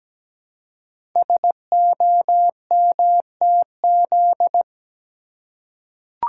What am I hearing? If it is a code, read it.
Morse code, heard as a keyed tone, about 17 wpm, 700 Hz. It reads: SOMTZ
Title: Code Groups - 5 Letter:Numbers - 17wpm